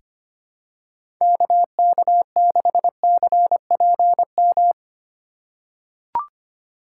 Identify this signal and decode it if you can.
Morse code, heard as a keyed tone, about 25 wpm, 700 Hz. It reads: KK6CPM
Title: USA Call Signs - 6 Characters - 3x Character Spacing - 25wpm